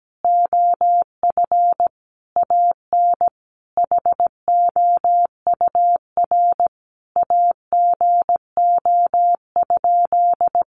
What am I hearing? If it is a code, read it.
Morse code, heard as a keyed tone, about 17 wpm, 700 Hz. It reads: OF AN HOUR AGO?
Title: Sentences from Top 1300 Words 17wpm